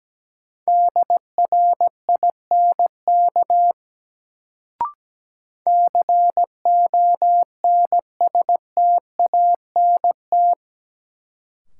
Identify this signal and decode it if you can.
Morse code, heard as a keyed tone, about 17 wpm, 700 Hz. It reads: DRINK CONSTANT
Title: Sentences from Top 1500 Words - New Word Drill Alt - 17wpm